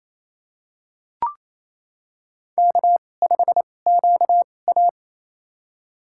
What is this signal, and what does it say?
Morse code, heard as a keyed tone, about 28 wpm, spaced as for 20 wpm, 700 Hz. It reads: K5QA
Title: USA Call Signs - 2x Character Spacing - 28wpm